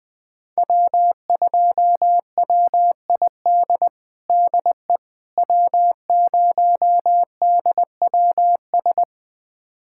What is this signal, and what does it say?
Morse code, heard as a keyed tone, about 20 wpm, 700 Hz. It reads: W2WID DE W0DWS
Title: USA Callsign DE Callsign - 5 Characters - 20wpm